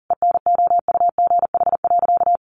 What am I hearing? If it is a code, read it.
Morse code, heard as a keyed tone, about 40 wpm, 700 Hz. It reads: ENOUGH.
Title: Sentences from Top 1400 Words 40wpm